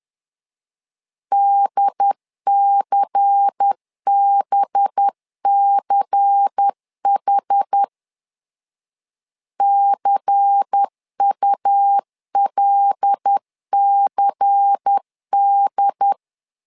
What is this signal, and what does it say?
Morse code, heard as a keyed tone, about 10 wpm, 800 Hz. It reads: DCBCH CULCD
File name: LEZIONI / 5 LBVC / 5 LBVC 10wpm 2349